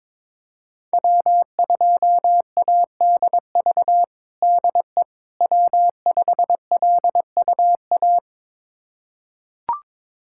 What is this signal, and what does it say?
Morse code, heard as a keyed tone, about 22 wpm, 700 Hz. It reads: W2ADV DE W5LUA
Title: USA Callsign DE Callsign - 5 Characters - 22wpm